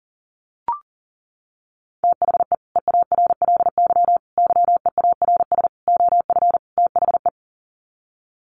Morse code, heard as a keyed tone, about 40 wpm, 700 Hz. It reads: THE EARLY YEARS OF THE
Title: Sets of 5 Words Encore 40wpm